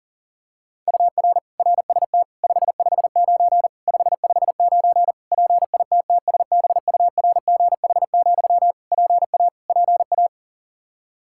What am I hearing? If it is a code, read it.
Morse code, heard as a keyed tone, about 40 wpm, 700 Hz. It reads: UR RST 559 559 PITTSBURGH, PA PA
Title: Practice QSO - Exchange 1 - Beginning - 40wpm